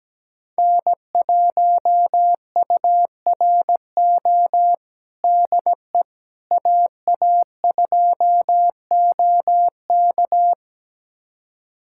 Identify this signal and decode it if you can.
Morse code, heard as a keyed tone, about 17 wpm, 700 Hz. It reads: N1URO DE AA2OK
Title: USA Callsign DE Callsign - 5 Characters - 17wpm